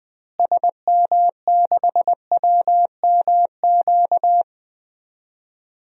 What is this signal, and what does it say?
Morse code, heard as a keyed tone, about 20 wpm, 700 Hz. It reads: SM6WMQ
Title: DXCC-100 Call Signs - 6 Characters - Repeated 3x - 20wpm